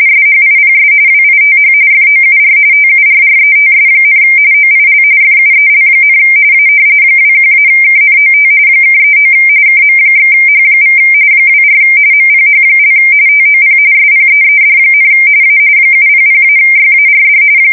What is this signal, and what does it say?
RTTY, about 45.45 baud. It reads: SHIPS WITH FOUR TURRETS, OIL FUEL, A 21-KNOT (39 KM/H= 24 MPH) TOP SPEED, A 700-YARD
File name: fsk170 / battleships